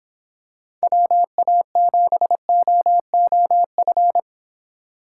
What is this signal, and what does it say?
Morse code, heard as a keyed tone, about 26 wpm, 700 Hz. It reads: WA7OOF